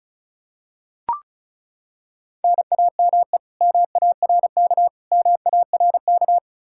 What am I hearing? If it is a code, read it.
Morse code, heard as a keyed tone, about 36 wpm, 700 Hz. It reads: NAME MARK MARK
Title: Practice QSO - Exchange 1 - Beginning - 35wpm